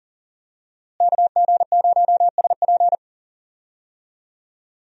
Morse code, heard as a keyed tone, about 40 wpm, 700 Hz. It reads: KG0SP
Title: USA Call Signs - 5 Characters - Repeated 3x - 40wpm